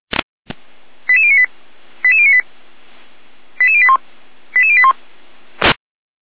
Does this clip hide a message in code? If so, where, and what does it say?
Yes: five-tone selective call (ZVEI) from 1.1 to 4.9 s, heard as quick runs of five tones, tones of about 70 ms. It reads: 88098 88098 88081 88081